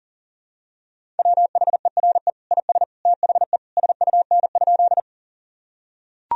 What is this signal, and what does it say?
Morse code, heard as a keyed tone, about 40 wpm, 700 Hz. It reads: WHERE IS THE SUN?
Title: Sentences from Top 1000 Words - Review Part 1 - 40wpm